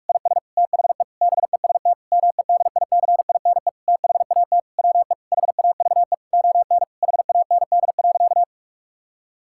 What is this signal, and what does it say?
Morse code, heard as a keyed tone, about 46 wpm, 700 Hz. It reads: IS THE BEST MEDICINE THAT WE HAVE ON HAND.
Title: Sentences from Top 1000 Words - Review Part 6 - 45wpm